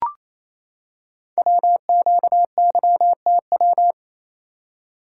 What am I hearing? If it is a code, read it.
Morse code, heard as a keyed tone, about 27 wpm, 700 Hz. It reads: WQYTW
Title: Code Groups - 5 Letters - 28wpm